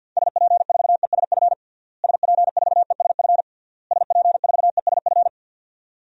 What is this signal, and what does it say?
Morse code, heard as a keyed tone, about 53 wpm, 700 Hz. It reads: SP4ESF SP4ESF SP4ESF
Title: DXCC-100 Call Signs - 6 Characters - Repeated 3x - 50wpm